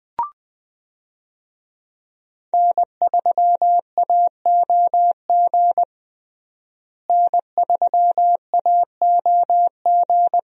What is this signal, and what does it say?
Morse code, heard as a keyed tone, about 21 wpm, 700 Hz. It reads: N3AOG N3AOG
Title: USA Call Signs - 5 Characters - Repeated 2x - 20wpm